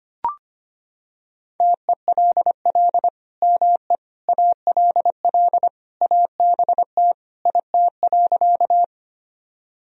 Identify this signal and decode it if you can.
Morse code, heard as a keyed tone, about 25 wpm, 700 Hz. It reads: TELL ME ALL ABT IT.
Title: Sentences from Top 200 Words and Abbreviations 25wpm